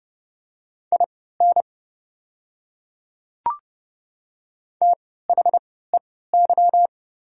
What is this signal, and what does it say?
Morse code, heard as a keyed tone, about 30 wpm, spaced as for 17 wpm, 700 Hz. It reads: IN THEY
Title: 3x Character Spacing - Top 100 Words - 30wpm